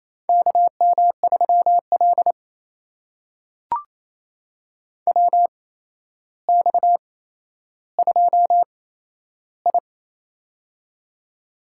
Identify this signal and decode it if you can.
Morse code, heard as a keyed tone, about 27 wpm, spaced as for 12 wpm, 700 Hz. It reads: KM3L WX2I
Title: USA Call Signs - 4 Characters - 8x Character Spacing - 28wpm